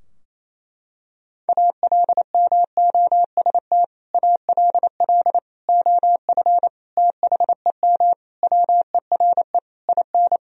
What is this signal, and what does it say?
Morse code, heard as a keyed tone, about 28 wpm, 700 Hz. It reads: ALMOST ALL OF THEM WERE IN
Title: Sentences from Top 1000 Words - Review Part 4 - 28wpm